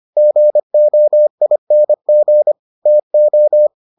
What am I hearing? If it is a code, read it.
Morse code, heard as a keyed tone, about 25 wpm, 600 Hz. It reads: GOING TO